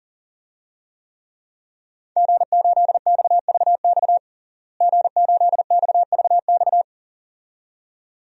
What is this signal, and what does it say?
Morse code, heard as a keyed tone, about 40 wpm, 700 Hz. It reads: G8XVX G8XVX
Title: DXCC-100 Call Signs - 5 Characters - Repeated 2x - 40wpm